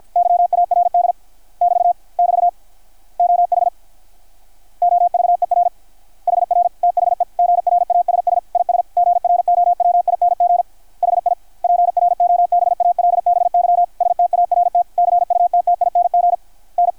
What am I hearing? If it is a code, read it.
Morse code, heard as a keyed tone, about 53 wpm, 690 Hz. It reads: YARD <BT> <BT> OH OVER HR THE GRASS IS GROWING HI PROBALBY START CUTTING N
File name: archived / CRSnet